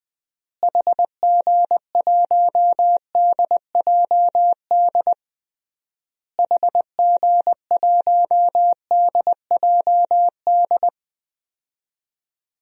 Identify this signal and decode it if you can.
Morse code, heard as a keyed tone, about 20 wpm, 700 Hz. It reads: HG1DJD HG1DJD
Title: DXCC-100 Call Signs - 6 Characters - Repeated 3x - 20wpm